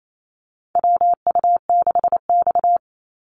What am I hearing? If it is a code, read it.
Morse code, heard as a keyed tone, about 28 wpm, 700 Hz. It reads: WU6X
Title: Call Signs - 28wpm